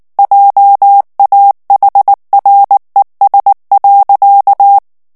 Morse code, heard as a keyed tone, about 19 wpm, 800 Hz. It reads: JAHRES.
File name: morse mp3 / ALLEMAND texte clair 2 vitesse 18 mots minute